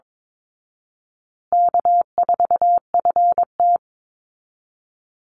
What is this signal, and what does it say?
Morse code, heard as a keyed tone, about 22 wpm, 700 Hz. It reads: K4FT
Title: Easy Call Signs Repeated 2x - 22wpm